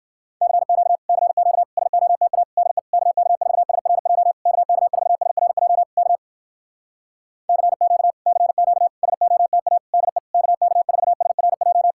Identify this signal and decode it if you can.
Morse code, heard as a keyed tone, about 59 wpm, 700 Hz. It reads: CQ CQ SOTA DE KK4SRJ KK4SRJ K CQ CQ SOTA DE KK4SRJ